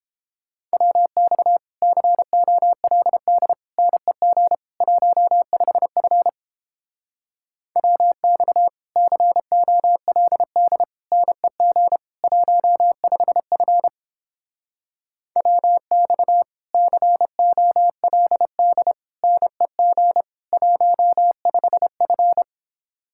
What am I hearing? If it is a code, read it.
Morse code, heard as a keyed tone, about 31 wpm, 700 Hz. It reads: WX COLD NEG 15F WX COLD NEG 15F WX COLD NEG 15F